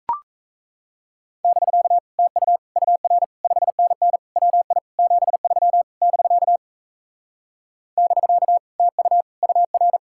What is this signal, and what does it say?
Morse code, heard as a keyed tone, about 38 wpm, 700 Hz. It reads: <BK> TU UR 5NN WI 73 <BK> <BK> TU UR